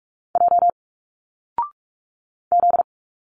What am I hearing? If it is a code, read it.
Morse code, heard as a keyed tone, about 46 wpm, 700 Hz. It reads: J Z
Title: Single Letters 45wpm